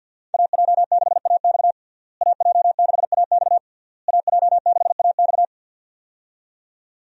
Morse code, heard as a keyed tone, about 46 wpm, 700 Hz. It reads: AJ6AX AJ6AX AJ6AX
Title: USA Call Signs - 5 Characters - Repeated 3x - 50wpm